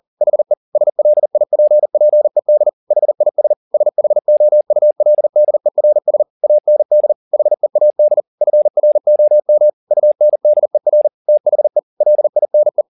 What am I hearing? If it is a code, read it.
Morse code, heard as a keyed tone, about 40 wpm, 600 Hz. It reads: HE SLIPPED HIS SHOULDERS AND HEAD FROM UNDER THE LINE
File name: the old man and the sea 2 40wpm